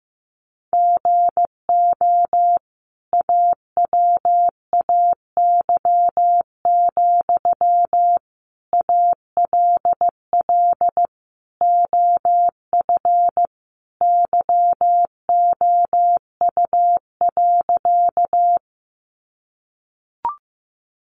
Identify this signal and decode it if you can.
Morse code, heard as a keyed tone, about 15 wpm, 700 Hz. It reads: GO AWAY, ALL OF YOU.
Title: Sentences from Top 600 Words 15wpm